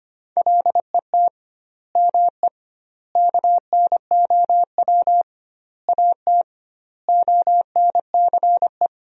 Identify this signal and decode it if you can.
Morse code, heard as a keyed tone, about 25 wpm, 700 Hz. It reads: LET ME KNOW AT ONCE
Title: 2x Word-Spacing - Sentences from Top 300 Words - 25wpm